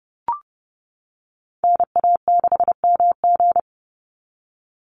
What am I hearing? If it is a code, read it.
Morse code, heard as a keyed tone, about 30 wpm, 700 Hz. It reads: NA6MG
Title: NA SOTA Activator Callsigns 30wpm